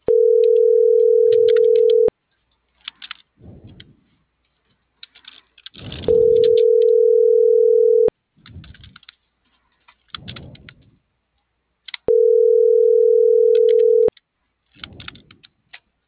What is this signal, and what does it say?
Ringback tone (North America), 440 + 480 Hz.